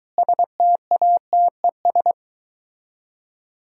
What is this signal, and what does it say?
Morse code, heard as a keyed tone, about 23 wpm, 700 Hz. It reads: STATES